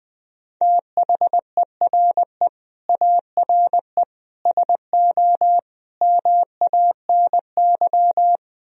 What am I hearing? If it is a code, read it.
Morse code, heard as a keyed tone, about 20 wpm, 700 Hz. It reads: THERE ARE SO MANY
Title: Speed-Racing - Sets of 4 Words 17wpm